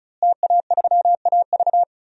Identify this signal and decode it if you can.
Morse code, heard as a keyed tone, about 36 wpm, 700 Hz. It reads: TA3AV
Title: DXCC-100 Call Signs - 5 Characters - Repeated 2x - 35wpm